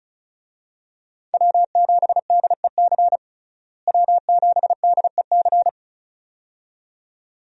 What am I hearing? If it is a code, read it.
Morse code, heard as a keyed tone, about 36 wpm, 700 Hz. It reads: W7DEC W7DEC
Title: USA Call Signs - 5 Characters - Repeated 2x - 35wpm